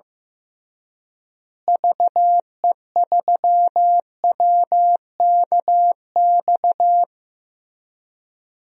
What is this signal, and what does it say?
Morse code, heard as a keyed tone, about 15 wpm, 700 Hz. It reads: VE3WKX
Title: DXCC-100 Call Signs - 6 Characters - Repeated 3x - 15wpm